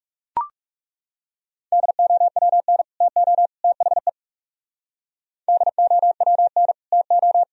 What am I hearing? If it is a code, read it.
Morse code, heard as a keyed tone, about 38 wpm, 700 Hz. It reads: DOWN TO THE DOWN TO